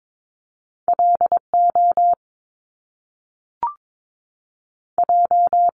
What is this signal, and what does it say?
Morse code, heard as a keyed tone, about 22 wpm, spaced as for 3 wpm, 700 Hz. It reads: L OJ
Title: Top Two Letter Combinations in Words 22wpm